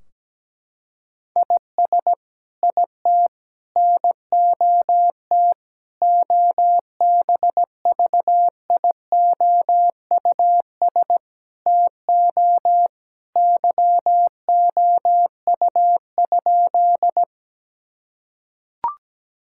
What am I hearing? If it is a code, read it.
Morse code, heard as a keyed tone, about 17 wpm, 700 Hz. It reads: IS IT NOT OBVIOUS TO YOU?
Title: Sentences from Top 1600 Words 17wpm